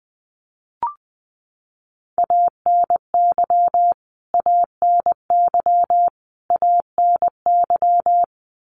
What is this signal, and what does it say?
Morse code, heard as a keyed tone, about 20 wpm, 700 Hz. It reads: ANY ANY ANY
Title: Top 500 Words Repeated 20wpm